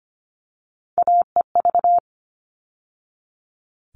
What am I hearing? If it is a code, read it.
Morse code, heard as a keyed tone, about 25 wpm, 700 Hz. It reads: AEV
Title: All Cut Numbers - 3-Digit 25wpm